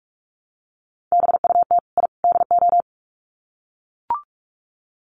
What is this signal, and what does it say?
Morse code, heard as a keyed tone, about 44 wpm, 700 Hz. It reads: BUT I DO
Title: Sets of 3 Words 45wpm